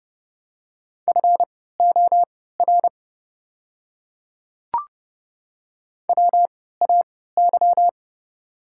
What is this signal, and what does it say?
Morse code, heard as a keyed tone, about 31 wpm, spaced as for 17 wpm, 700 Hz. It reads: FOR WAY
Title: 3x Character Spacing - Top 100 Words - 30wpm